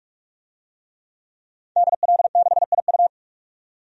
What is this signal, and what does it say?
Morse code, heard as a keyed tone, about 46 wpm, 700 Hz. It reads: DL6IU
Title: DXCC-100 Call Signs - 5 Characters - 45wpm